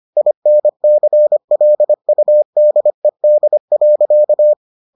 Morse code, heard as a keyed tone, about 25 wpm, 600 Hz. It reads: INCLUDED.